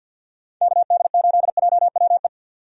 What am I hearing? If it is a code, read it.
Morse code, heard as a keyed tone, about 53 wpm, 700 Hz. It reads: KD8JWE
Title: USA Call Signs - 6 Characters - 50wpm